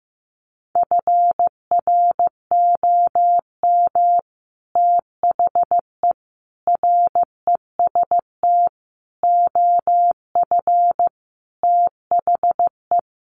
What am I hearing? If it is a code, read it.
Morse code, heard as a keyed tone, about 15 wpm, 700 Hz. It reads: FROM THE REST OF THE
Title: Sets of 5 Words 15wpm